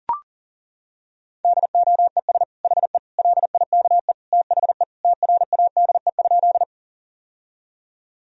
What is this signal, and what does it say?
Morse code, heard as a keyed tone, about 37 wpm, 700 Hz. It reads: DOES HE LIKE THE TRADE?
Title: Sentences from Top 1000 Words - Review Part 4 - 40wpm